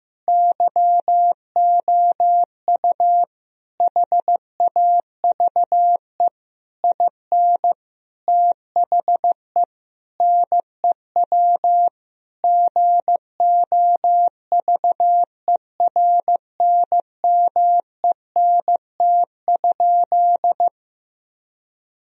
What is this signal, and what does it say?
Morse code, heard as a keyed tone, about 15 wpm, 700 Hz. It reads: YOU HAVE IN THE NEW GOVERNMENT?
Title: Sentences from Top 1000 Words - Review Part 3 - 15wpm